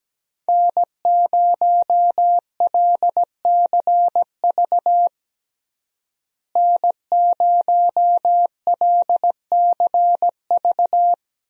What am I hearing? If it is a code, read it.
Morse code, heard as a keyed tone, about 17 wpm, 700 Hz. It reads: N0LCV N0LCV
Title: USA Call Signs - 5 Characters - Repeated 3x - 17wpm